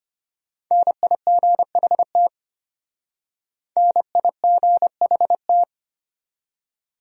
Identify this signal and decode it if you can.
Morse code, heard as a keyed tone, about 32 wpm, 700 Hz. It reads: NIGHT NIGHT